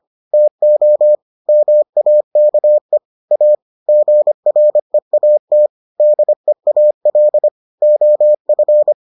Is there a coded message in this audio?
Yes: Morse code from 0.3 to 8.9 s, heard as a keyed tone, about 25 wpm, 600 Hz. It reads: TO MAKE A GREAT DEAL OF